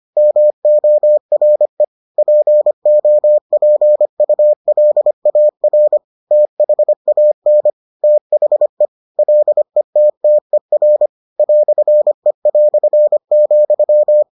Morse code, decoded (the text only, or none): MORE POPULAR THAN THE LETTER "E",